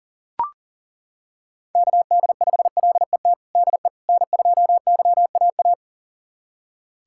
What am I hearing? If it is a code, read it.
Morse code, heard as a keyed tone, about 40 wpm, 700 Hz. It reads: KD5LET DE N2YAA
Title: USA Callsign DE Callsign 40wpm